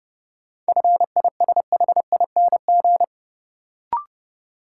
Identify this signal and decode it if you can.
Morse code, heard as a keyed tone, about 30 wpm, 700 Hz. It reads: FISHING
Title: Sentences from Top 1000 Words - Review Part 5 - New Word Drill - 30wpm